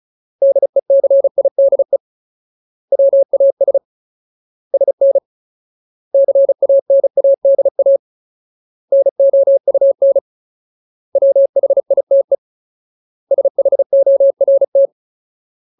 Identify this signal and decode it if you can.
Morse code, heard as a keyed tone, about 34 wpm, 550 Hz. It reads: DECIDE WAS SN CANADA NOUN WHITE SHORT
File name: Ultimate Word List / 35WPM Ultimate